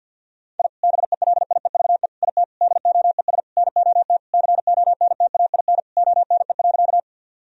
Morse code, heard as a keyed tone, about 50 wpm, 700 Hz. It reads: I BELIEVE IT DOES NOT CONTAIN ONE.
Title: Sentences from Top 1000 Words - Review Part 2 - 50wpm